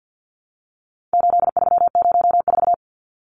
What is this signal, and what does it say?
Morse code, heard as a keyed tone, about 48 wpm, 700 Hz. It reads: 8304
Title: Numbers - 4-Digit 50wpm